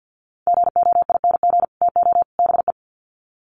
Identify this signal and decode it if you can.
Morse code, heard as a keyed tone, about 50 wpm, 700 Hz. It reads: GOING TO BE
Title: Sets of 3 Words 50wpm